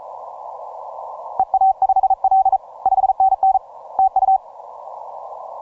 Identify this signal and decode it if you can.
Morse code, heard as a keyed tone, about 33 wpm, 770 Hz. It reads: EA5L 5NN TU